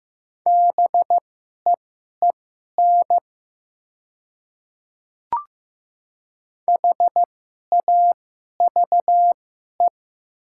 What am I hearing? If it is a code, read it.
Morse code, heard as a keyed tone, about 15 wpm, spaced as for 11 wpm, 690 Hz. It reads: BEEN HAVE
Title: 2x Character Spacing - Top 100 Words - 15wpm